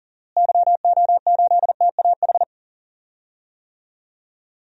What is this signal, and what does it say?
Morse code, heard as a keyed tone, about 40 wpm, 700 Hz. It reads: YO8TAH
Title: DXCC-100 Call Signs - 6 Characters - Repeated 2x - 40wpm